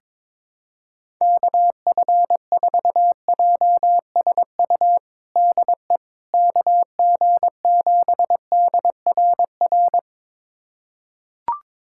Morse code, heard as a keyed tone, about 22 wpm, 700 Hz. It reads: KF4JSU DE KG7DRR
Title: USA Callsign DE Callsign 22wpm